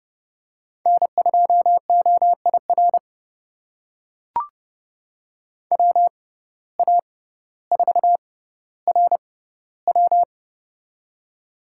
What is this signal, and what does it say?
Morse code, heard as a keyed tone, about 30 wpm, spaced as for 15 wpm, 700 Hz. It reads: N2OIR WA4RW
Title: USA Call Signs - 5 Characters - 6x Character Spacing - 30wpm